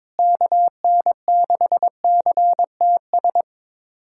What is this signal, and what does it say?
Morse code, heard as a keyed tone, about 22 wpm, 700 Hz. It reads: KN6CTS
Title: USA Callsign DE Callsign 22wpm